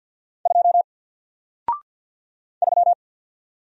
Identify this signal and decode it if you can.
Morse code, heard as a keyed tone, about 57 wpm, 700 Hz. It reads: 2 3
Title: Single Letter-Number Alt 50wpm